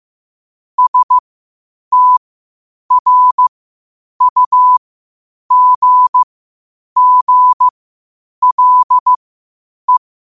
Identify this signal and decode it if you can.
Morse code, heard as a keyed tone, about 15 wpm, spaced as for 8 wpm, 1000 Hz. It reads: STRUGGLE
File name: qotdfiles / 8wpm